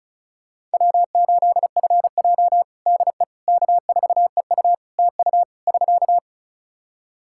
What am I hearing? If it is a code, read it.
Morse code, heard as a keyed tone, about 36 wpm, 700 Hz. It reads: W8FJ DE K4EU TU <SK>
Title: Practice QSO - Exchange 4 - Ending - 35wpm